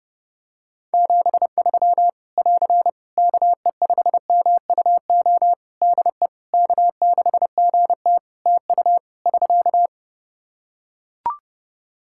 Morse code, heard as a keyed tone, about 30 wpm, 700 Hz. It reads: 73 <AR> KE5MUO DE K6GT TU <SK>
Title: Practice QSO - Exchange 4 - Ending - 30wpm